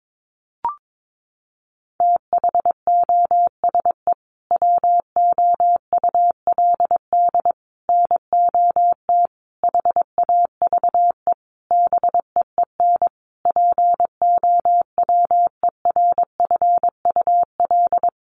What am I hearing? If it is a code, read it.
Morse code, heard as a keyed tone, about 22 wpm, 700 Hz. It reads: THOSE WOULD NOT HAVE BEEN POWERFUL
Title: Sentences from Top 1300 Words 22wpm